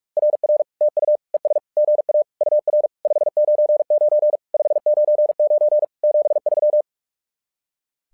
Morse code, heard as a keyed tone, about 46 wpm, 600 Hz. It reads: RR TU ES GA UR 599 599 73
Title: SOTA QSO Practice 45wpm